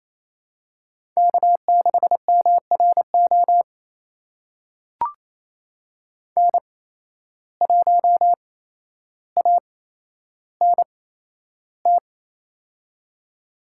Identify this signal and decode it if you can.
Morse code, heard as a keyed tone, about 28 wpm, spaced as for 12 wpm, 700 Hz. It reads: K6MRO N1ANT
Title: USA Call Signs - 5 Characters - 8x Character Spacing - 28wpm